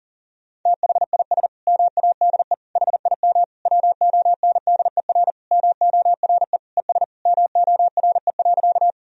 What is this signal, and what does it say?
Morse code, heard as a keyed tone, about 40 wpm, 700 Hz. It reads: THIS MADE HIM WONDER MORE ES MORE.
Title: Sentences from Top 500 Words and Abbreviations 40wpm